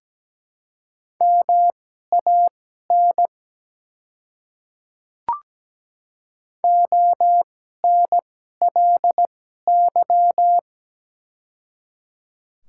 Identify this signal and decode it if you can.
Morse code, heard as a keyed tone, about 17 wpm, spaced as for 12 wpm, 700 Hz. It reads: MAN ONLY
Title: 2x Character Spacing - Top 300 Words - 17wpm